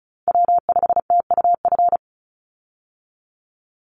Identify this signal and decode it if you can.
Morse code, heard as a keyed tone, about 34 wpm, 700 Hz. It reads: W5TUF
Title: Call Signs Repeated 3x - 35wpm